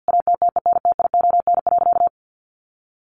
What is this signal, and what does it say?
Morse code, heard as a keyed tone, about 50 wpm, 700 Hz. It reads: ATTENTION.
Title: Sentences from Top 1000 Words 50wpm